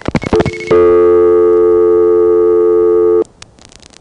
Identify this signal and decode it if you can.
Dial tone (North America), 350 + 440 Hz.